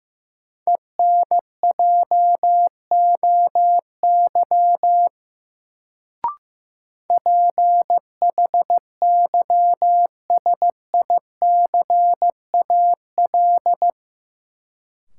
Morse code, heard as a keyed tone, about 15 wpm, 700 Hz. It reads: ENJOY PHYSICAL
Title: Sentences from Top 1000 Words - Review Part 4 - New Word Drill - 15wpm